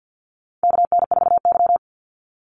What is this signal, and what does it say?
Morse code, heard as a keyed tone, about 50 wpm, 700 Hz. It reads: KN4Y
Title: Call Signs Repeated 2x - 50wpm